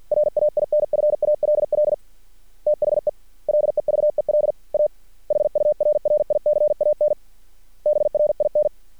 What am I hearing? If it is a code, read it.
Morse code, heard as a keyed tone, about 50 wpm, 600 Hz. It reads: RAINFALL THE LEVEL A HURRICAN BRIN
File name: archived / CRSnet